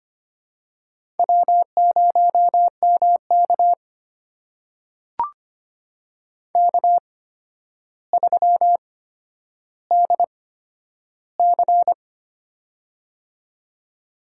Morse code, heard as a keyed tone, about 25 wpm, spaced as for 11 wpm, 700 Hz. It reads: W0MK K3DC
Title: USA Call Signs - 4 Characters - 8x Character Spacing - 25wpm